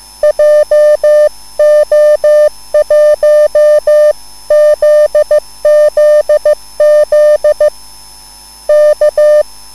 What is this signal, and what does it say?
Morse code, heard as a keyed tone, about 15 wpm, 600 Hz. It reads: JO1ZZZ K
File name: QSL / CW lesson